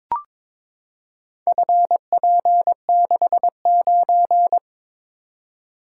Morse code, heard as a keyed tone, about 22 wpm, 700 Hz. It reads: FP69
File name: practice / 22wpm / Four Letter-Number Combinations - 22wpm